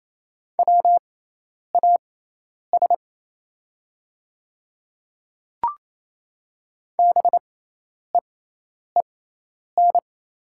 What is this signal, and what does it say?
Morse code, heard as a keyed tone, about 28 wpm, spaced as for 10 wpm, 700 Hz. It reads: WAS BEEN
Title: 6x Character Spacing - Top 300 Words - 28wpm